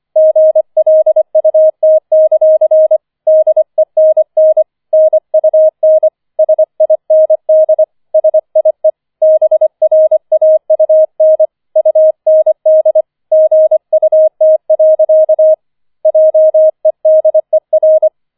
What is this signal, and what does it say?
Morse code, heard as a keyed tone, about 25 wpm, 630 Hz. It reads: GLUT; DENN NUN SIND SIE BRAUN UND GUT. JEDER
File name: 25wpm